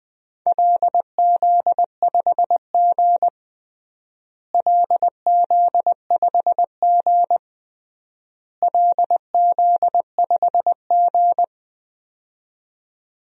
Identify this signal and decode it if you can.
Morse code, heard as a keyed tone, about 20 wpm, 700 Hz. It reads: LZ5G LZ5G LZ5G
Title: DXCC-100 Call Signs - 4 Characters - Repeated 3x - 20wpm